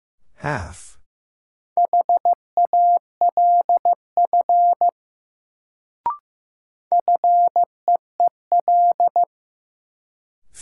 Morse code, heard as a keyed tone, about 15 wpm, 700 Hz. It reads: HALF FEEL
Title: Sentences from Top 400 Words - New Word Drill - 15wpm